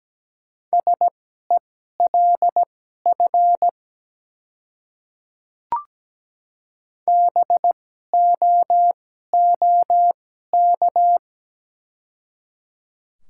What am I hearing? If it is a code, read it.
Morse code, heard as a keyed tone, about 17 wpm, spaced as for 12 wpm, 690 Hz. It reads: SELF BOOK
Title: 2x Character Spacing - Top 300 Words - 17wpm